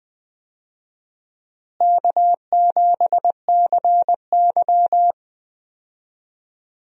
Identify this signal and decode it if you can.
Morse code, heard as a keyed tone, about 20 wpm, 700 Hz. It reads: K7CY
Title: USA Call Signs - 4 Characters - Repeated 3x - 20wpm